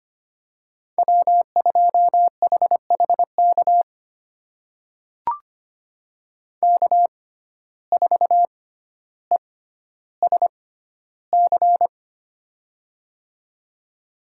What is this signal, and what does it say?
Morse code, heard as a keyed tone, about 25 wpm, spaced as for 13 wpm, 700 Hz. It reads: W2HHK K4ESC
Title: USA Call Signs - 5 Characters - 6x Character Spacing - 25wpm